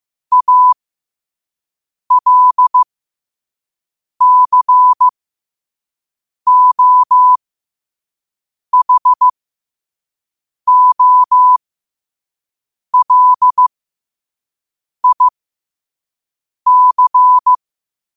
Morse code, heard as a keyed tone, about 15 wpm, spaced as for 5 wpm, 1000 Hz. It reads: ALCOHOLIC
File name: qotdfiles / 5wpm